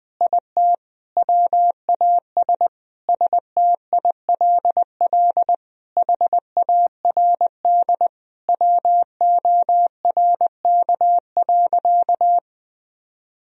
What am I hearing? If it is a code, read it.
Morse code, heard as a keyed tone, about 20 wpm, 700 Hz. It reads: IT WAS STILL HARD WORK.
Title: Sentences from Top 1000 Words - Review Part 3 - 20wpm